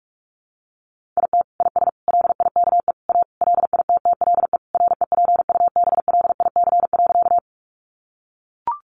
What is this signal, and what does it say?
Morse code, heard as a keyed tone, about 44 wpm, 700 Hz. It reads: IT IS LIKE A LITTLE REPUBLIC.
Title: Sentences from Top 1500 Words 45wpm